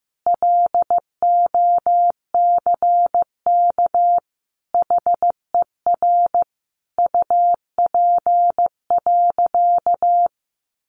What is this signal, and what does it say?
Morse code, heard as a keyed tone, about 15 wpm, 700 Hz. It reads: LOCK HER UP.
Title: Sentences from Top 1300 Words 15wpm